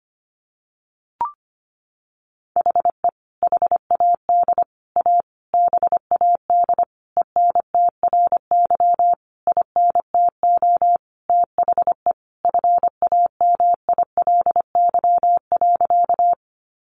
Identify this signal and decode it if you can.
Morse code, heard as a keyed tone, about 25 wpm, 700 Hz. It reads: HE HAD A BAD ENTRY INTO THE FAMILY.
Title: Sentences from Top 1500 Words 25wpm